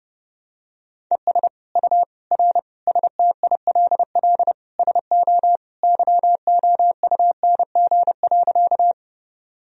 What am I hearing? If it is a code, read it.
Morse code, heard as a keyed tone, about 30 wpm, 700 Hz. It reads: ES U R STILL SO YOUNG.
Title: Sentences from Top 400 Words and Abbreviations 30wpm